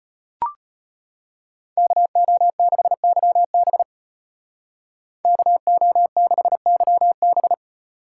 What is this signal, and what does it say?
Morse code, heard as a keyed tone, about 38 wpm, 700 Hz. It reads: KO6YB KO6YB